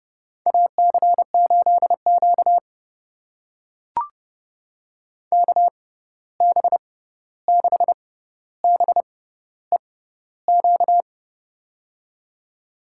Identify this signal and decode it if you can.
Morse code, heard as a keyed tone, about 30 wpm, spaced as for 14 wpm, 700 Hz. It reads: AC8Q KB6BEQ
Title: USA Call Signs - 6x Character Spacing - 30wpm